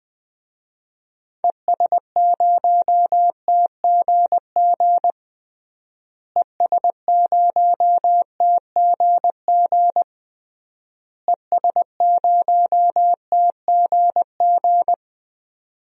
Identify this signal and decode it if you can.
Morse code, heard as a keyed tone, about 20 wpm, 700 Hz. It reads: ES0TGG ES0TGG ES0TGG
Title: DXCC-100 Call Signs - 6 Characters - Repeated 3x - 20wpm